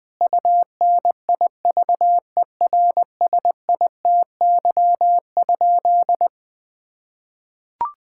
Morse code, heard as a keyed tone, about 20 wpm, 700 Hz. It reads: UNIVERSITY?
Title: Sentences from Top 1000 Words - Review Part 5 - 20wpm